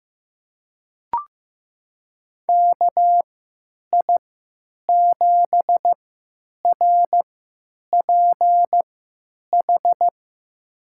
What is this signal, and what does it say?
Morse code, heard as a keyed tone, about 15 wpm, spaced as for 9 wpm, 700 Hz. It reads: KI7RPH
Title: USA Call Signs - 6 Characters - 3x Character Spacing - 15wpm